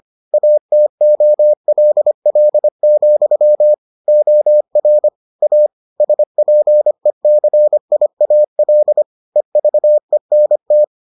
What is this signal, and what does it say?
Morse code, heard as a keyed tone, about 25 wpm, 600 Hz. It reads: ATOLL, OR A SPECIAL EVENT